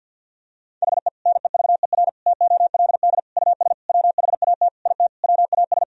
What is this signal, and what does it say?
Morse code, heard as a keyed tone, about 51 wpm, 700 Hz. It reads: HE NEVER TOLD US WHAT IT WAS
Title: Sentences from Top 400 Words and Abbreviations 50wpm